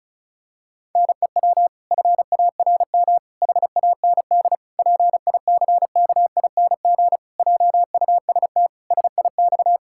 Morse code, heard as a keyed tone, about 36 wpm, 700 Hz. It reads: NEW FARM HAND PICKING JUST SIX
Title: Pangram Challenge v2 - 35wpm